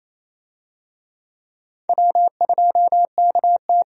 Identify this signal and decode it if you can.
Morse code, heard as a keyed tone, about 28 wpm, 700 Hz. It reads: W2KT
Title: USA Call Signs - 4 Characters - 28wpm